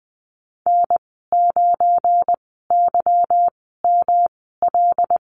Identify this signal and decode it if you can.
Morse code, heard as a keyed tone, about 19 wpm, 700 Hz. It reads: N 9 Y M L
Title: 179 Lesson 29g - Slash - Call Signs 20wpm